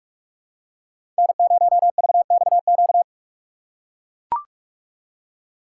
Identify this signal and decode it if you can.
Morse code, heard as a keyed tone, about 44 wpm, 690 Hz. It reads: N0VXQ
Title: USA Call Signs - 5 Characters - 6x Character Spacing - 45wpm